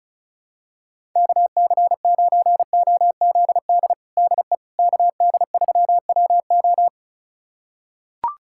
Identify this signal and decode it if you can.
Morse code, heard as a keyed tone, about 36 wpm, 700 Hz. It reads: KC9OZD DE KD3WO
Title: USA Callsign DE Callsign 35wpm